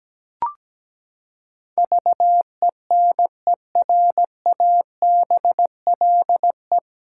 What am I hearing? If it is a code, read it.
Morse code, heard as a keyed tone, about 17 wpm, 700 Hz. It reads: VENERABLE
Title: Pangram Challenge v2 - 17wpm